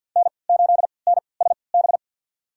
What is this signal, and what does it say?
Morse code, heard as a keyed tone, about 53 wpm, spaced as for 29 wpm, 700 Hz. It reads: N8NSB
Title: USA Call Signs - 5 Characters - 3x Character Spacing - 50wpm